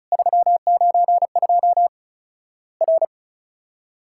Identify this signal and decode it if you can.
Morse code, heard as a keyed tone, about 34 wpm, 700 Hz. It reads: 392 R
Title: Worked All Europe DX Contest - QTC Exchange - 35wpm